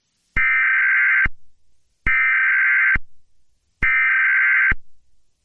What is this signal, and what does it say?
Emergency Alert System (SAME) tones (header bursts), heard as harsh buzzing bursts. It is heard from 0.4 to 4.7 s.
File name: u-s-eas-system-full